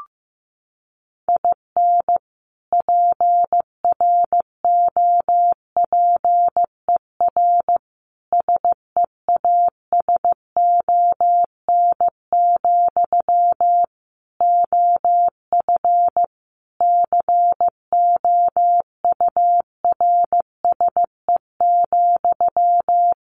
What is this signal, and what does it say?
Morse code, heard as a keyed tone, about 14 wpm, 700 Hz. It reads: IN PROPER SEASON, OF COURSE,
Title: Sentences from Top 1400 Words 15wpm